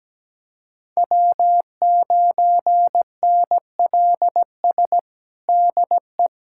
Morse code, heard as a keyed tone, about 17 wpm, 700 Hz. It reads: W9NLS DE
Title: USA Callsign DE Callsign 17wpm